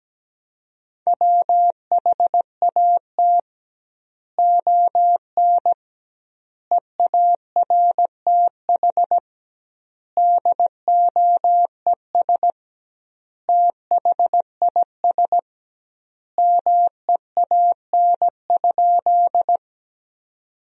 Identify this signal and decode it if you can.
Morse code, heard as a keyed tone, about 17 wpm, 700 Hz. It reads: WHAT ON EARTH DOES THIS MEAN?
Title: 2x Word-Spacing - Sentences from Top 200 Words - 17wpm